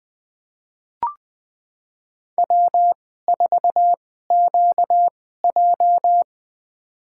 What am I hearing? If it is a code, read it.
Morse code, heard as a keyed tone, about 20 wpm, spaced as for 14 wpm, 700 Hz. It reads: W4QJ
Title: USA Call Signs - 2x Character Spacing - 20wpm